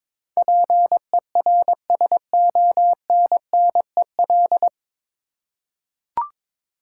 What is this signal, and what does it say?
Morse code, heard as a keyed tone, about 22 wpm, 700 Hz. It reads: PERSONNEL
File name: 9 Letter Words - First 3 Letters Given - 22wpm